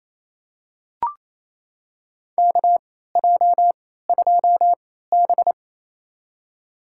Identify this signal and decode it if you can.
Morse code, heard as a keyed tone, about 28 wpm, spaced as for 16 wpm, 700 Hz. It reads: KJ2B
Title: USA Call Signs - 3x Character Spacing - 28wpm